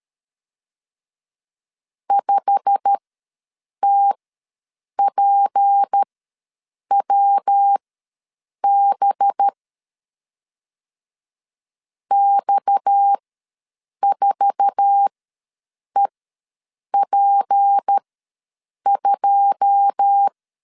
Morse code, heard as a keyed tone, about 13 wpm, spaced as for 7 wpm, 800 Hz. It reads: 5TPWB X4EP2
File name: LEZIONI / 9 2 ws 12wpm 3406